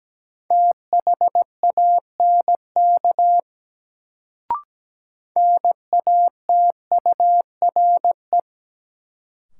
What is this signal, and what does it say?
Morse code, heard as a keyed tone, about 17 wpm, 700 Hz. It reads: THANK NATURE
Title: Sentences from Top 600 Words - New Word Drill - 17wpm